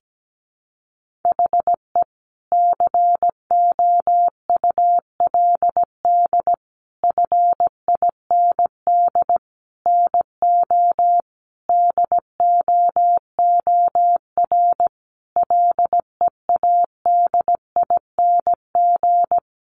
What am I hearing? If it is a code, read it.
Morse code, heard as a keyed tone, about 17 wpm, 700 Hz. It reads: HE COULD FIND NO DOOR LEADING
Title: Sentences from Top 1100 Words 17wpm